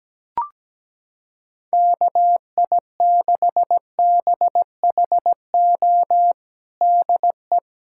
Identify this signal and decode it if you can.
Morse code, heard as a keyed tone, about 17 wpm, 700 Hz. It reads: KI6BHO DE
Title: USA Callsign DE Callsign 17wpm